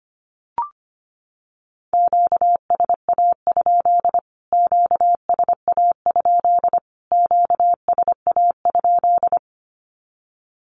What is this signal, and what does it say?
Morse code, heard as a keyed tone, about 25 wpm, 700 Hz. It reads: QSA? QSA? QSA?
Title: Q-Codes Repeated 3x 25wpm